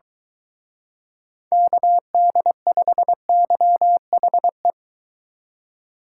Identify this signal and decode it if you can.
Morse code, heard as a keyed tone, about 22 wpm, 700 Hz. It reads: KD5YHE